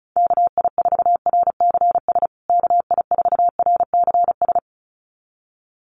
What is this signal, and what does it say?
Morse code, heard as a keyed tone, about 34 wpm, 700 Hz. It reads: KI4RCS KI4RCS
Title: USA Call Signs Repeated 2x 35wpm